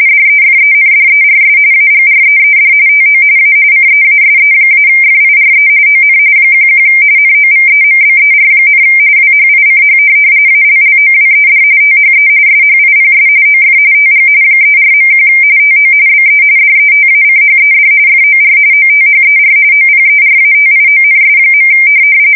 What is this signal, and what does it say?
RTTY, about 45.45 baud. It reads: DEFENDERS AGREED TO A CEASE-FIRE ON 11 NOVEMBER, WHICH ALLOWED THE SHIPS OF TG 34.1 TO BE DETACHED FOR OTHER OPERATIONS.